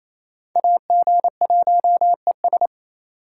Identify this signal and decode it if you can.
Morse code, heard as a keyed tone, about 38 wpm, 700 Hz. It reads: AG1ES